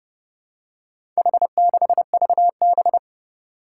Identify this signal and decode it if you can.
Morse code, heard as a keyed tone, about 30 wpm, 690 Hz. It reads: H6VB
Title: Four Challenging Letter-Number Combinations 30wpm